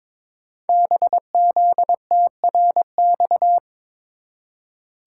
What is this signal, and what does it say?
Morse code, heard as a keyed tone, about 22 wpm, 700 Hz. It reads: BZTRX
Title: Code Groups - 5 Letter:Numbers - 22wpm